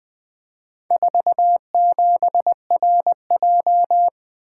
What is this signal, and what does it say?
Morse code, heard as a keyed tone, about 20 wpm, 700 Hz. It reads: 47RJ
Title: Four Letter-Number Combinations - 20wpm